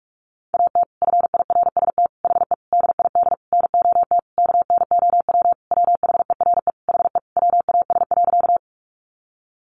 Morse code, heard as a keyed tone, about 46 wpm, 700 Hz. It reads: AT FIRST HE DID NOT KNOW WHERE HE WAS.
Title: Sentences from Top 600 Words 45wpm